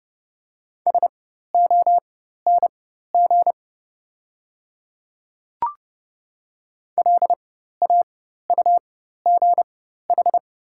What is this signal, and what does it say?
Morse code, heard as a keyed tone, about 30 wpm, spaced as for 14 wpm, 700 Hz. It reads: SONG LAUGH
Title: 4x Character Spacing - Top 500 Words - 30wpm